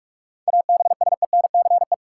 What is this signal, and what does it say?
Morse code, heard as a keyed tone, about 46 wpm, 700 Hz. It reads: ABSENCE
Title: Sets of 4 Words - New Word Drill - 45wpm